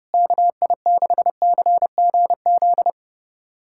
Morse code, heard as a keyed tone, about 30 wpm, 700 Hz. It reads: KI6CGZ
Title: USA Callsign DE Callsign - 6 Characters - 30wpm